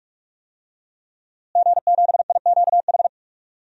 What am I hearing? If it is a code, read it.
Morse code, heard as a keyed tone, about 46 wpm, 700 Hz. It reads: G7IQH
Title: DXCC-100 Call Signs - 5 Characters - Repeated 2x - 45wpm